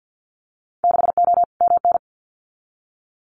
Morse code, heard as a keyed tone, about 50 wpm, 700 Hz. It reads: 6O MN
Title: Winter Field Day Exchanges Encore 50wpm